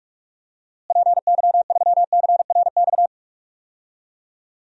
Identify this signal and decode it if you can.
Morse code, heard as a keyed tone, about 45 wpm, 700 Hz. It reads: PY3CRX
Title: DXCC-100 Call Signs - 6 Characters - Repeated 3x - 45wpm